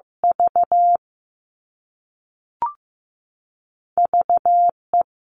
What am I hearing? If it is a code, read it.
Morse code, heard as a keyed tone, about 15 wpm, 700 Hz. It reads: V VE
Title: All Cut Numbers - 2-Digit 15wpm